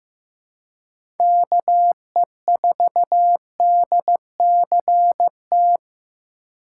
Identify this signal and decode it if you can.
Morse code, heard as a keyed tone, about 15 wpm, 700 Hz. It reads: KE4DCT
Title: USA Call Signs - 6 Characters - 3x Character Spacing - 15wpm